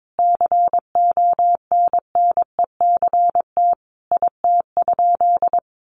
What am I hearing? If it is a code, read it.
Morse code, heard as a keyed tone, about 22 wpm, 700 Hz. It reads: CONNECT IT?
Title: Sentences from Top 1500 Words 22wpm